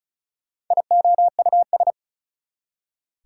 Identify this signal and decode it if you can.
Morse code, heard as a keyed tone, about 45 wpm, 700 Hz. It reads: IOUS